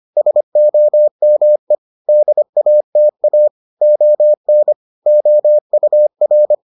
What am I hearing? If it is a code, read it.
Morse code, heard as a keyed tone, about 25 wpm, 600 Hz. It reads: SOME DATA ON OUR